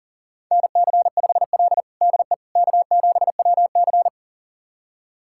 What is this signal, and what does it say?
Morse code, heard as a keyed tone, about 40 wpm, 700 Hz. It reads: NC5L DE K7WC
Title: USA Callsign DE Callsign - 4 Characters - 40wpm